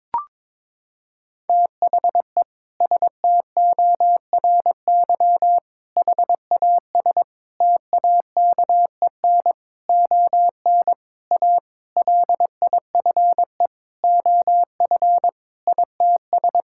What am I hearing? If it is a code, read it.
Morse code, heard as a keyed tone, about 22 wpm, 700 Hz. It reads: THE STORY HAS TAKEN ON A LIFE OF ITS
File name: practice / 22wpm / Sentences with 7-Word Phrases - 22wpm